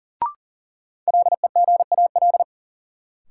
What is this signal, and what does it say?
Morse code, heard as a keyed tone, about 40 wpm, 700 Hz. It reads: LEGAL
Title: Sentences from Top 1200 Words - New Word Drill Alt - 40wpm